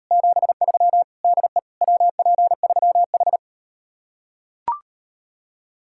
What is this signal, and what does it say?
Morse code, heard as a keyed tone, about 39 wpm, 700 Hz. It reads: 73 DE WP3H